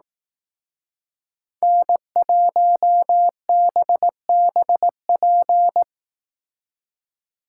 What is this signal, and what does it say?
Morse code, heard as a keyed tone, about 18 wpm, 700 Hz. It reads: N1BBP
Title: Speed Racing - USA 1x3 Call Signs 15wpm